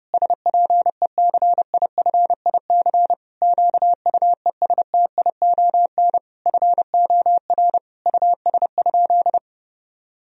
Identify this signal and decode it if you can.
Morse code, heard as a keyed tone, about 32 wpm, 700 Hz. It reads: SPECIFIC QUESTION FOR US?
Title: Sentences from Top 1000 Words - Review Part 3 - 30wpm